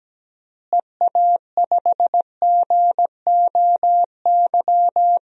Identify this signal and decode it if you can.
Morse code, heard as a keyed tone, about 17 wpm, 700 Hz. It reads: EA5GOY
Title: DXCC-100 Call Signs - 6 Characters - Repeated 3x - 17wpm